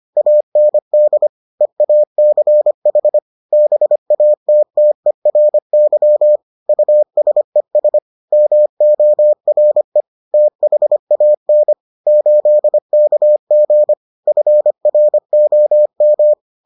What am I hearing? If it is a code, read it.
Morse code, heard as a keyed tone, about 25 wpm, 600 Hz. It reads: AND EACH BATTERY USES MORE THAN 8KG FROM